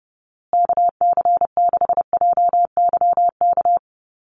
Morse code, heard as a keyed tone, about 30 wpm, 700 Hz. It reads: KC6JYK
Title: USA Call Signs Repeated 3x 30wpm